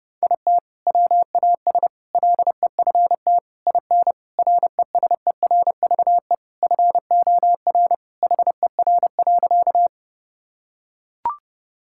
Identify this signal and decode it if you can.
Morse code, heard as a keyed tone, about 32 wpm, 700 Hz. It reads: IT WAS LEFT IN RESERVE FOR HER.
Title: Sentences from Top 1000 Words - Review Part 6 - 30wpm